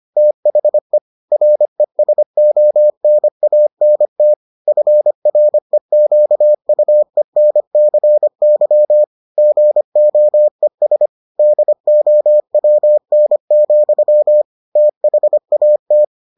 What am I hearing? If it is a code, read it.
Morse code, heard as a keyed tone, about 25 wpm, 600 Hz. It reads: THE RESONANT FREQUENCY GOES DOWN, THAT